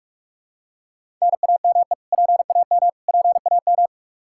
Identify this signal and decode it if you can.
Morse code, heard as a keyed tone, about 46 wpm, 700 Hz. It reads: NAME PAM PAM
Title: Practice QSO - Exchange 1 - Beginning - 45wpm